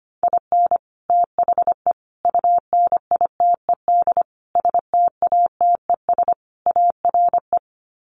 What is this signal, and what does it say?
Morse code, heard as a keyed tone, about 25 wpm, 700 Hz. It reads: IN THE UNITED STATES ARE
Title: Sets of 5 Words Encore 25wpm